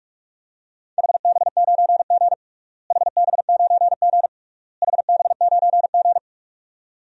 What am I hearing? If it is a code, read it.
Morse code, heard as a keyed tone, about 46 wpm, 700 Hz. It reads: HB9G HB9G HB9G
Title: DXCC-100 Call Signs - 4 Characters - Repeated 3x - 45wpm